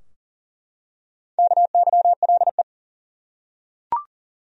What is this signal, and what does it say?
Morse code, heard as a keyed tone, about 40 wpm, 700 Hz. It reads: KYLE
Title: USA Common Names - Max 4 Letters - 40wpm